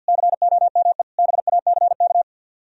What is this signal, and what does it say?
Morse code, heard as a keyed tone, about 60 wpm, 700 Hz. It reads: COME BACK
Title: Speed-Racing - Sets of 5 Words 50wpm